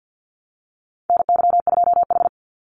Morse code, heard as a keyed tone, about 50 wpm, 700 Hz. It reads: NY2H
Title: Call Signs Repeated 2x - 50wpm